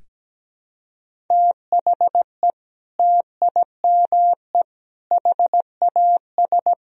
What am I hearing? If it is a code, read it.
Morse code, heard as a keyed tone, about 17 wpm, 700 Hz. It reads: THE TIME HAS
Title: Sentences from Top 1000 Words - Review Part 1 - 17wpm